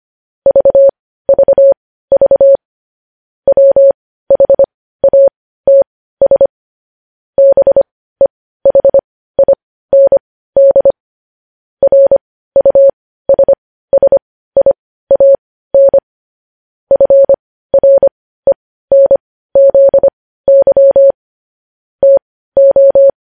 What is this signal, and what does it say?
Morse code, heard as a keyed tone, about 25 wpm, spaced as for 15 wpm, 550 Hz. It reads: VVV WHATS BEHIND RUSSIAN FRENZY TO
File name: podcast / News headlines-1x-25x15